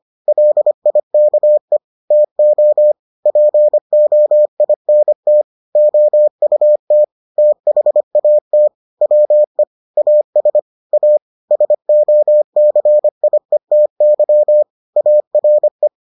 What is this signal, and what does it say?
Morse code, heard as a keyed tone, about 25 wpm, 600 Hz. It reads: LIKE TO POINT OUT THAT WE AS A SOCIETY ARE